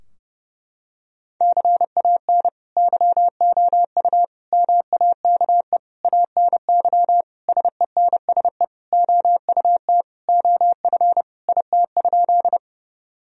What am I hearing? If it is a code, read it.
Morse code, heard as a keyed tone, about 30 wpm, 700 Hz. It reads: CAN YOU MAKE ANY SENSE OUT OF IT?
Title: Sentences from Top 1000 Words - Review Part 4 - 30wpm